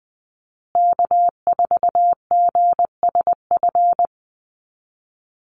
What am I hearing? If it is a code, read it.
Morse code, heard as a keyed tone, about 20 wpm, 700 Hz. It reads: K4GSF
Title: USA Call Signs Repeated 2x 20wpm